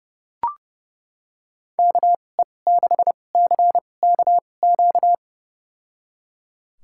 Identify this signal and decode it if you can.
Morse code, heard as a keyed tone, about 32 wpm, spaced as for 22 wpm, 700 Hz. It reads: KE6CKQ
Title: USA Call Signs - 6 Characters - 2x Character Spacing - 30wpm